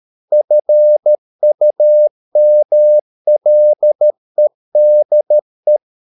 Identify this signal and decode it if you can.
Morse code, heard as a keyed tone, about 13 wpm, 600 Hz. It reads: FUMLEDE
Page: Ferie 13 wpm